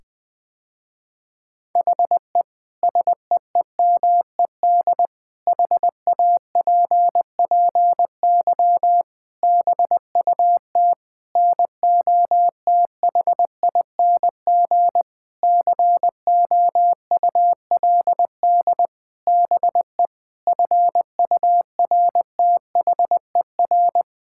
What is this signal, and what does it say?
Morse code, heard as a keyed tone, about 20 wpm, 700 Hz. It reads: HE SEEMED HAPPY BUT NOTHING COULD BE FURTHER
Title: Sentences with 8-Word Phrases - Spoken Beginning - 20wpm